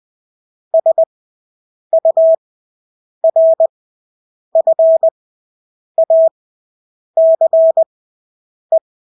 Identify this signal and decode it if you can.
Morse code, heard as a keyed tone, about 20 wpm, spaced as for 8 wpm, 650 Hz. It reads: SURFACE
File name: Engelske ord 20/8 wpm